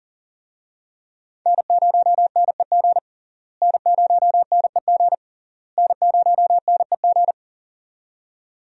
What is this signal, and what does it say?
Morse code, heard as a keyed tone, about 40 wpm, 700 Hz. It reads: N0NEG N0NEG N0NEG
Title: USA Call Signs - 5 Characters - Repeated 3x - 40wpm